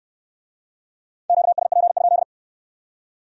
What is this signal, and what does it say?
Morse code, heard as a keyed tone, about 65 wpm, 700 Hz. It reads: QSP?